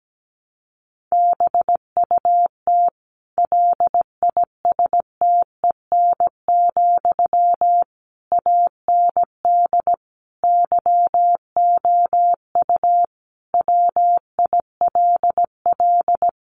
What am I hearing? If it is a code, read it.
Morse code, heard as a keyed tone, about 17 wpm, 700 Hz. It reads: BUT LISTEN, AND YOU WILL
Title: Sentences from Top 600 Words 17wpm